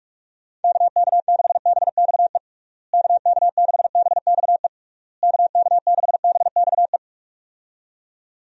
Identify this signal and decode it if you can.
Morse code, heard as a keyed tone, about 46 wpm, 700 Hz. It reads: KK6BXE KK6BXE KK6BXE
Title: USA Call Signs - 6 Characters - Repeated 3x - 45wpm